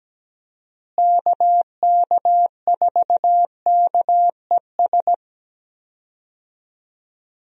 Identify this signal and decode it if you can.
Morse code, heard as a keyed tone, about 17 wpm, 700 Hz. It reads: KK4KES
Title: USA Call Signs - 6 Characters - Repeated 2x - 17wpm